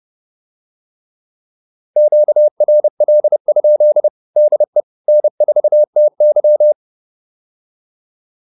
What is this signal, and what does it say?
Morse code, heard as a keyed tone, about 30 wpm, 600 Hz. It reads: QRL? DE N4TY
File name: QSO109 30